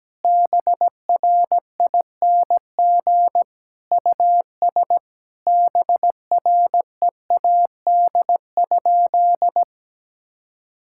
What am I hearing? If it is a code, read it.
Morse code, heard as a keyed tone, about 17 wpm, 700 Hz. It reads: BRING US BREAD?
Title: Sentences from Top 1600 Words 17wpm